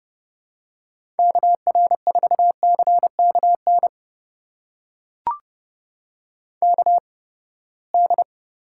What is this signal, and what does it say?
Morse code, heard as a keyed tone, about 30 wpm, spaced as for 21 wpm, 700 Hz. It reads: KR4CKN KD
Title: USA Call Signs - 6 Characters - 8x Character Spacing - 30wpm